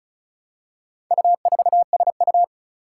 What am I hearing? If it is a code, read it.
Morse code, heard as a keyed tone, about 36 wpm, 700 Hz. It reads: U4SU
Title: Four Challenging Letter-Number Combinations 35wpm